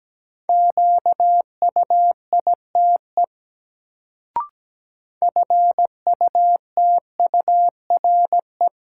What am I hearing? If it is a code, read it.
Morse code, heard as a keyed tone, about 17 wpm, 700 Hz. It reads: QUITE FUTURE
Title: Sentences from Top 700 Words - New Word Drill - 17wpm